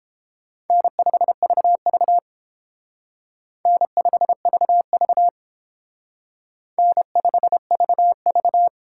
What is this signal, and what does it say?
Morse code, heard as a keyed tone, about 31 wpm, 700 Hz. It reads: N5VV N5VV N5VV